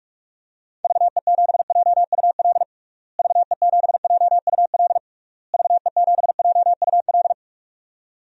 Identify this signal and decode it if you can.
Morse code, heard as a keyed tone, about 46 wpm, 700 Hz. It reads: VE7JUL VE7JUL VE7JUL
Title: DXCC-100 Call Signs - 6 Characters - Repeated 3x - 45wpm